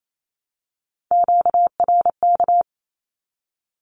Morse code, heard as a keyed tone, about 28 wpm, 700 Hz. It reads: QRK
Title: Q-Codes 28wpm